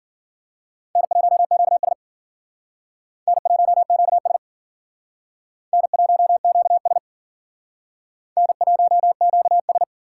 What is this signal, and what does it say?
Morse code, heard as a keyed tone, about 51 wpm, 700 Hz. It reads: N1QS N1QS N1QS N1QS